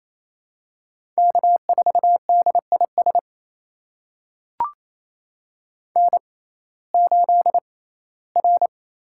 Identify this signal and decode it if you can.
Morse code, heard as a keyed tone, about 28 wpm, spaced as for 17 wpm, 700 Hz. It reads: K4DIS N8R
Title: USA Call Signs - 5 Characters - 6x Character Spacing - 28wpm